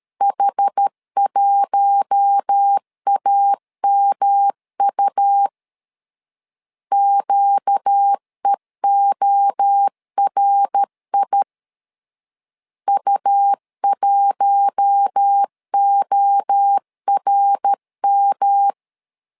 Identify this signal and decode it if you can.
Morse code, heard as a keyed tone, about 13 wpm, 800 Hz. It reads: H1AMU QEORI U1ORM
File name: LEZIONI / 4 Q1 12wpm 2028